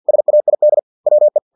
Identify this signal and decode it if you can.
Morse code, heard as a keyed tone, about 50 wpm, 600 Hz. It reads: SAID WE
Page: Never 49wpm